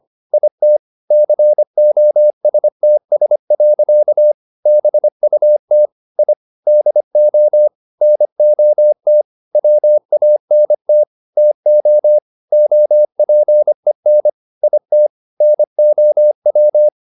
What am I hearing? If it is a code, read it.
Morse code, heard as a keyed tone, about 25 wpm, 600 Hz. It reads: IT COSTS. BUT I DO NOT WANT TO OPEN IT NOW
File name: the old man and the sea part 1 25wpm